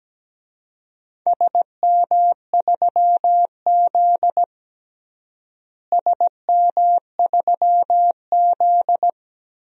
Morse code, heard as a keyed tone, about 17 wpm, 700 Hz. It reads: SM3Z SM3Z
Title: DXCC-100 Call Signs - 4 Characters - Repeated 2x - 17wpm